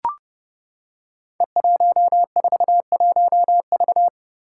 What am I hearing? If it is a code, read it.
Morse code, heard as a keyed tone, about 31 wpm, 700 Hz. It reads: E141V
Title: Five Challenging Letter-Number Combinations 30wpm